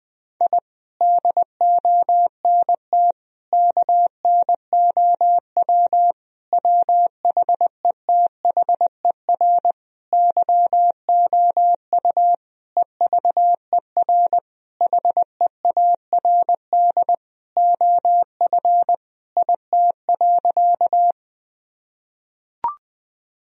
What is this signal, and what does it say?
Morse code, heard as a keyed tone, about 20 wpm, 700 Hz. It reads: I DONT KNOW WHETHER YOU EVER HEARD OF IT.
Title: Sentences from Top 1000 Words - Review Part 3 - 20wpm